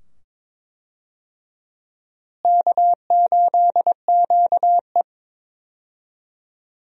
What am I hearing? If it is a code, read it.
Morse code, heard as a keyed tone, about 22 wpm, 700 Hz. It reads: K8QE
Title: USA Call Signs - 4 Characters - Repeated 2x - 22wpm